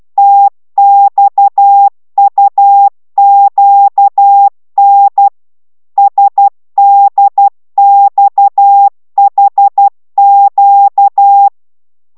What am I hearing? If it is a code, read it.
Morse code, heard as a keyed tone, about 12 wpm, 800 Hz. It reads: TXUQN SDXHQ